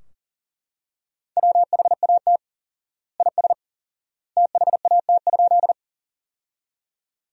Morse code, heard as a keyed tone, about 40 wpm, 700 Hz. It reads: WHAT IS THAT?
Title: 4x Word-Spacing - Sentences from Top 100 Words - 40wpm